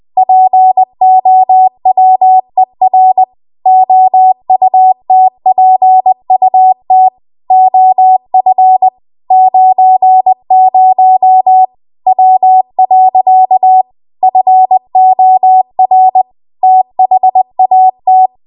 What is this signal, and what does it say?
Morse code, heard as a keyed tone, about 20 wpm, 750 Hz. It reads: POWER OUTPUT OF 90 W. FOR THAT
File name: Archive / 20 WPM / 20WPM